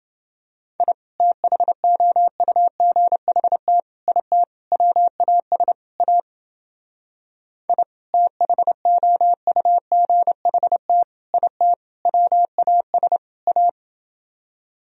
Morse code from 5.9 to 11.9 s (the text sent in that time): A I THOUGHT IT